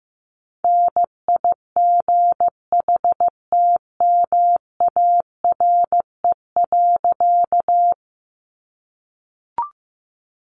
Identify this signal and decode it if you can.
Morse code, heard as a keyed tone, about 15 wpm, 700 Hz. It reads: NIGHTMARE.
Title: A Sound Of Thunder 15wpm